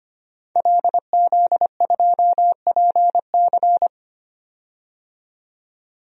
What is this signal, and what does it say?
Morse code, heard as a keyed tone, about 25 wpm, 700 Hz. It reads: LZ2PC
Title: DXCC-100 Call Signs - 5 Characters - 25wpm